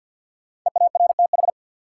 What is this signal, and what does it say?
Morse code, heard as a keyed tone, about 49 wpm, 700 Hz. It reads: EARTH
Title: Sets of 5 Words Encore - New Word Drill - 50wpm